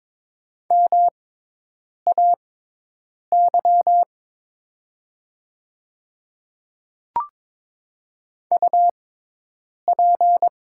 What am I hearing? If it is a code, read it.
Morse code, heard as a keyed tone, about 22 wpm, spaced as for 8 wpm, 700 Hz. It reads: MAY UP
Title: 6x Character Spacing - Top 100 Words - 22wpm